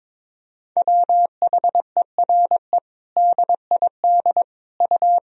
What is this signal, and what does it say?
Morse code, heard as a keyed tone, about 22 wpm, 700 Hz. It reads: WHERE DID U
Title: Sentences from Top 200 Words and Abbreviations 22wpm